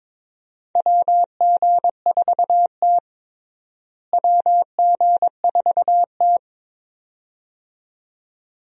Morse code, heard as a keyed tone, about 22 wpm, 700 Hz. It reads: WG4T WG4T
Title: USA Call Signs - 4 Characters - Repeated 2x - 22wpm